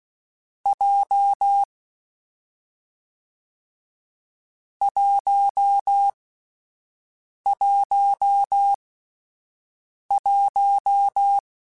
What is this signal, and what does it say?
Morse code, heard as a keyed tone, about 16 wpm, 800 Hz. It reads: J 1 1 1
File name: Difficult Combos / Difficult Combos-mp3s / WPJ WPJ1 @10